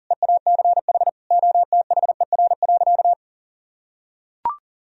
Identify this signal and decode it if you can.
Morse code, heard as a keyed tone, about 40 wpm, 700 Hz. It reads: EACH OTHER.
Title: Sentences from Top 1000 Words - Review Part 3 - 40wpm